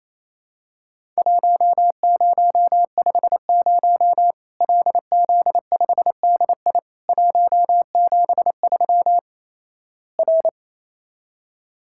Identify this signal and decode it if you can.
Morse code, heard as a keyed tone, about 28 wpm, 700 Hz. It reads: 1050 LZ5DI 173 R
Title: Worked All Europe DX Contest - QTC Exchange - 28wpm